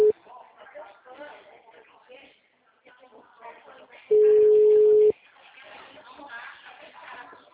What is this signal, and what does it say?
Ringback tone (Europe, 425 Hz), with speech.
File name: out- -2006- - - .43976